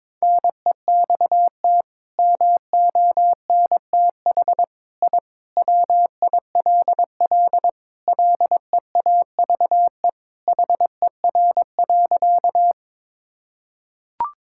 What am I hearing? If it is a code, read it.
Morse code, heard as a keyed tone, about 21 wpm, 700 Hz. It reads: NEXT MONTH I WILL LEAVE HER.
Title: Sentences from Top 1000 Words - Review Part 5 - 22wpm